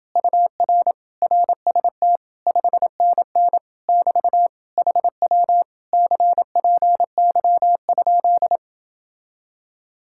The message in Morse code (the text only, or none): UR RST 5NN <BT> HW CPY?